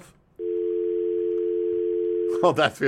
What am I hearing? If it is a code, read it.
Dial tone (North America), 350 + 440 Hz.